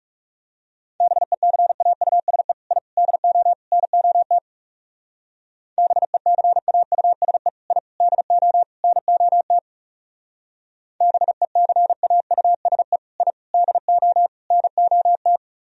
Morse code, heard as a keyed tone, about 34 wpm, 700 Hz. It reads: BECAUSE I DO NOT BECAUSE I DO NOT BECAUSE I DO NOT
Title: Speed-Racing - Sets of 4 Words 30wpm